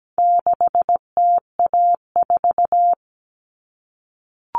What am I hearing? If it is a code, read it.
Morse code, heard as a keyed tone, about 17 wpm, 700 Hz. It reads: 6TA4
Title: All Cut Numbers - 4-Digit 17wpm